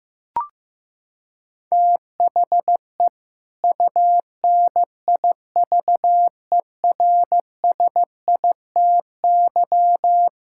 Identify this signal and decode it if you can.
Morse code, heard as a keyed tone, about 16 wpm, 700 Hz. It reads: THE UNIVERSITY
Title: Sentences from Top 1000 Words - Review Part 5 - 15wpm